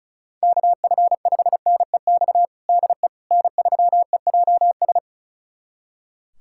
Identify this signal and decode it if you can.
Morse code, heard as a keyed tone, about 36 wpm, 700 Hz. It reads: KF5NEX DE N3EJS
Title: USA Callsign DE Callsign 35wpm